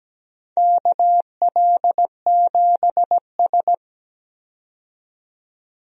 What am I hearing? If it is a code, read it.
Morse code, heard as a keyed tone, about 17 wpm, 700 Hz. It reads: KL7S
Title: DXCC-100 Call Signs - 4 Characters - 17wpm